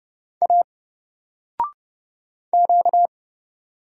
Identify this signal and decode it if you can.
Morse code, heard as a keyed tone, about 29 wpm, 700 Hz. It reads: A Q
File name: Single-Letter-Number-30wpm